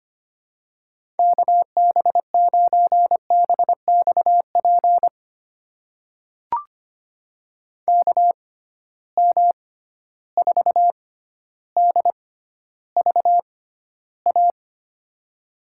Morse code, heard as a keyed tone, about 25 wpm, spaced as for 13 wpm, 700 Hz. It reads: KB9BXP KM4DVA
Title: USA Call Signs - 6 Characters - 6x Character Spacing - 25wpm